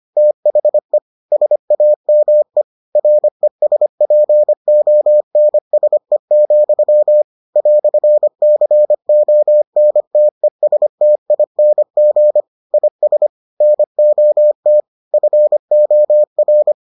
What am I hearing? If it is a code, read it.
Morse code, heard as a keyed tone, about 25 wpm, 600 Hz. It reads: THE SAME RESPONSE, "CONTESTING IS NOT FOR